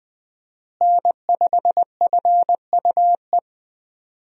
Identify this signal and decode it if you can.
Morse code, heard as a keyed tone, about 21 wpm, 700 Hz. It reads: N5FUE
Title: USA Call Signs - 5 Characters - Repeated 2x - 20wpm